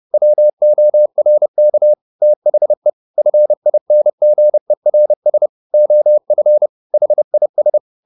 Morse code, heard as a keyed tone, about 30 wpm, 600 Hz. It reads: WORK THE FINGERS OF HIS
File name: the old man and the sea / the old man and the sea 1 30wpm